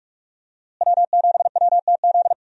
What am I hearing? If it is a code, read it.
Morse code, heard as a keyed tone, about 48 wpm, 700 Hz. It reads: W7WTZ
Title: USA Call Signs - 5 Characters - Repeated 3x - 45wpm